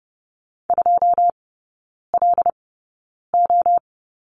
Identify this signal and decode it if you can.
Morse code, heard as a keyed tone, about 30 wpm, spaced as for 9 wpm, 700 Hz. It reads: 2LO
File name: 081 Lesson 14e - Number 2 - Three Characters 30wpm